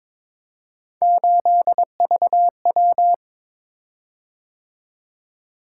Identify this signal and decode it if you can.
Morse code, heard as a keyed tone, about 22 wpm, 700 Hz. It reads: 8VW
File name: practice / 22wpm / Three Letter-Number Combinations - 22wpm